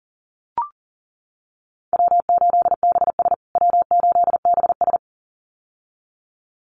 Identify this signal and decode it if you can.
Morse code, heard as a keyed tone, about 40 wpm, 700 Hz. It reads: W8BS W8BS
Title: USA Call Signs Repeated 2x 40wpm